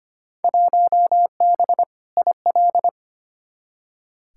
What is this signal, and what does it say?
Morse code, heard as a keyed tone, about 25 wpm, 700 Hz. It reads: 1B IL
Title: Field Day Exchanges Encore - 25wpm